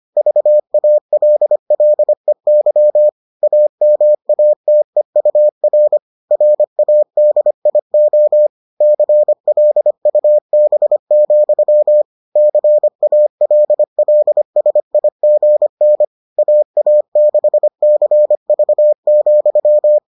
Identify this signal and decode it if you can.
Morse code, heard as a keyed tone, about 25 wpm, 600 Hz. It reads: VALLEY AMATEUR RADIO CLUB, CALLSIGN AA6CV,